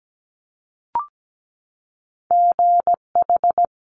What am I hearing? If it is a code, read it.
Morse code, heard as a keyed tone, about 17 wpm, 700 Hz. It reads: GH
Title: Top Two Letter Combinations in Words 17wpm